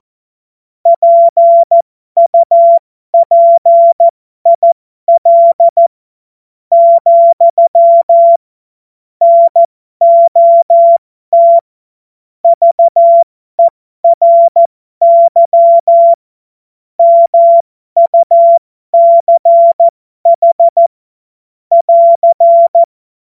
Morse code, heard as a keyed tone, about 14 wpm, spaced as for 11 wpm, 680 Hz. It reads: PUPIL, NOT VERY MUCH <AR>
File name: Morsekurs / Witze